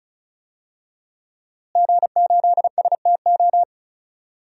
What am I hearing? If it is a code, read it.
Morse code, heard as a keyed tone, about 36 wpm, 700 Hz. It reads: G8STO
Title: DXCC-100 Call Signs - 5 Characters - Repeated 3x - 35wpm